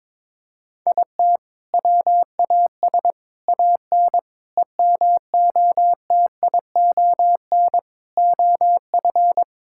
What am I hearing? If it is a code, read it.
Morse code, heard as a keyed tone, about 21 wpm, 700 Hz. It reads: IT WAS AN EMOTION OF
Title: Sentences from Top 1600 Words 22wpm